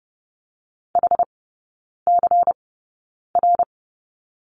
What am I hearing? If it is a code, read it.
Morse code, heard as a keyed tone, about 30 wpm, spaced as for 9 wpm, 700 Hz. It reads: HCR
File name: Lesson 18e - Letter m - Three Characters 30wpm